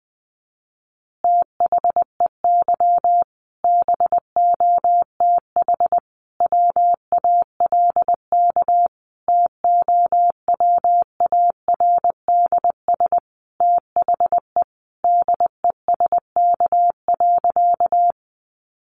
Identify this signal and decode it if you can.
Morse code, heard as a keyed tone, about 20 wpm, 700 Hz. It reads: THEY BOTH WALK TOWARDS THE DESK.
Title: Additional Easy Phrases 20wpm